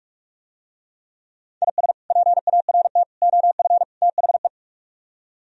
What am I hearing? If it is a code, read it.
Morse code, heard as a keyed tone, about 45 wpm, 700 Hz. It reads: IS PART OF THE